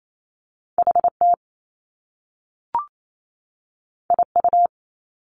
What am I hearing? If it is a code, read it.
Morse code, heard as a keyed tone, about 28 wpm, 700 Hz. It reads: HT IU
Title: Top Two Letter Combinations in Words 28wpm